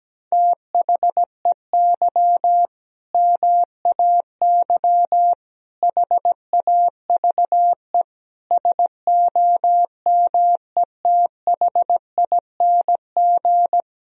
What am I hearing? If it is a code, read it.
Morse code, heard as a keyed tone, about 17 wpm, 700 Hz. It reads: THEY MAY HAVE SOMETHING
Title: Sentences from Top 1000 Words - Review Part 2 - 17wpm